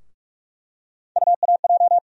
Morse code, heard as a keyed tone, about 45 wpm, 700 Hz. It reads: UAJ
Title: USA Call Sign Postfixes - 3 Letters 45wpm